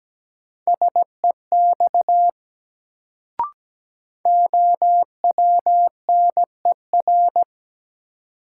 Morse code, heard as a keyed tone, about 17 wpm, 700 Hz. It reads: SEX OWNER
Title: Sentences from Top 1600 Words - New Word Drill Alt - 17wpm